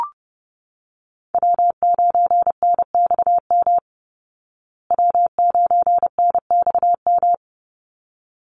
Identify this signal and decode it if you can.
Morse code, heard as a keyed tone, about 30 wpm, 700 Hz. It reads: W9NXM W9NXM
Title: Call Signs Repeated 3x - 30wpm